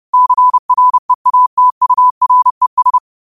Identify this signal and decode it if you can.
Morse code, heard as a keyed tone, about 30 wpm, 1000 Hz. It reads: CREATURES
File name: qotdfiles / 30wpm